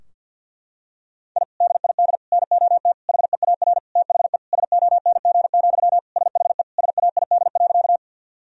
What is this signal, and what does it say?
Morse code, heard as a keyed tone, about 52 wpm, 700 Hz. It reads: I DID NOT HEAR THE SONG, SHE SAID.
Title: Sentences from Top 1000 Words - Review Part 2 - 50wpm